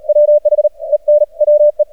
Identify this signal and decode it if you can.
Morse code, heard as a keyed tone, about 40 wpm, spaced as for 34 wpm, 600 Hz. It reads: WHEN WE
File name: archived / CRSnet